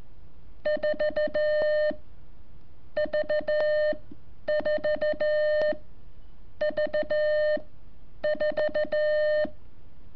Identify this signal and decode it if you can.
Morse code, heard as a keyed tone, about 16 wpm, spaced as for 8 wpm, 610 Hz. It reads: 4V4V4
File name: curso / ejercicio 23